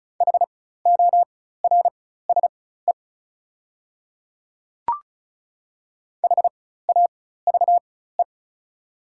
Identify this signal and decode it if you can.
Morse code, heard as a keyed tone, about 36 wpm, spaced as for 16 wpm, 690 Hz. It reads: HORSE HAVE
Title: 4x Character Spacing - Top 500 Words - 35wpm